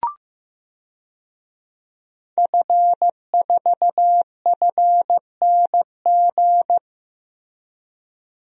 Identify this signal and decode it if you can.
Morse code, heard as a keyed tone, about 15 wpm, 700 Hz. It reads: F4FNG
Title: DXCC-100 Call Signs - 5 Characters - Repeated 2x - 15wpm